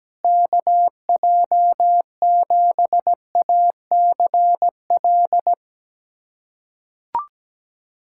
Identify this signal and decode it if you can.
Morse code, heard as a keyed tone, about 17 wpm, 700 Hz. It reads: KJ7ACL
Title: USA Callsign DE Callsign 17wpm